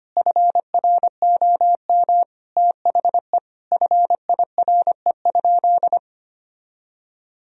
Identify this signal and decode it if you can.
Morse code, heard as a keyed tone, about 25 wpm, 700 Hz. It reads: FROM THE FIRE?
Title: Sentences from Top 1000 Words - Review Part 3 - 25wpm